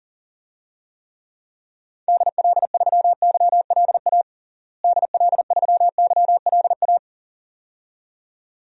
Morse code, heard as a keyed tone, about 40 wpm, 700 Hz. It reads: DL3YLA DL3YLA
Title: DXCC-100 Call Signs - 6 Characters - Repeated 2x - 40wpm